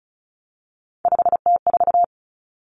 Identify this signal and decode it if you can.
Morse code, heard as a keyed tone, about 34 wpm, 700 Hz. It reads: 5T4
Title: Common Cut Numbers - 3-Digit 35wpm